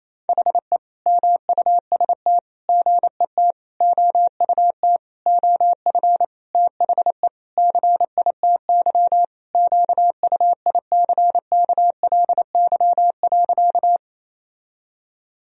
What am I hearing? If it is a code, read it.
Morse code, heard as a keyed tone, about 28 wpm, 700 Hz. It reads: HE MUST GET OUT OF THE CITY QUICKLY.
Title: Sentences from Top 1000 Words - Review Part 1 - 28wpm